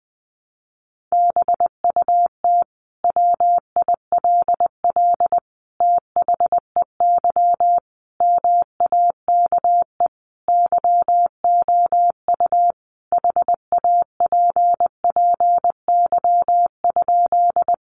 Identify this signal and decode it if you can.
Morse code, heard as a keyed tone, about 20 wpm, 700 Hz. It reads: BUT WILL THEY MAKE YOU HAPPY?
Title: Sentences from Top 1000 Words 20wpm